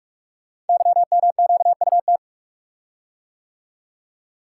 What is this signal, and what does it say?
Morse code, heard as a keyed tone, about 44 wpm, 690 Hz. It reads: YMQUT
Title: Code Groups - 5 Letters - 45wpm